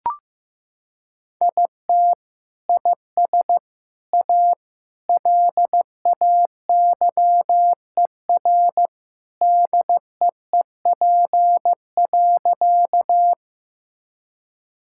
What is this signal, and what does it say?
Morse code, heard as a keyed tone, about 15 wpm, 700 Hz. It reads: IT IS A LAYER DEEP.
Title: Sentences from Top 1600 Words 15wpm